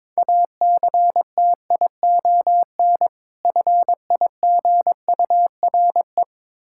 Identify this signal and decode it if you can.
Morse code, heard as a keyed tone, about 22 wpm, 700 Hz. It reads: ACTION FIGURE
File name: practice / 22wpm / Difficult Words with Hints - 22wpm